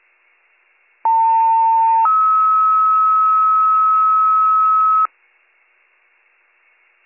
Two-tone page, heard as a one-second tone, then a three-second tone, about 900 then 1290 Hz. It is heard from 1.0 to 5.1 s.